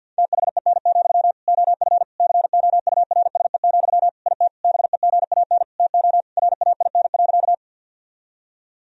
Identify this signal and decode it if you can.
Morse code, heard as a keyed tone, about 46 wpm, 700 Hz. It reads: THEN, OF COURSE, IT BEGAN TO RAIN.
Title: Sentences from Top 1000 Words - Review Part 1 - 50wpm